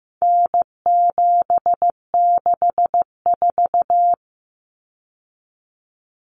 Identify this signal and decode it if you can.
Morse code, heard as a keyed tone, about 15 wpm, 700 Hz. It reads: N764
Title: Common Cut Numbers - 4-Digit 15wpm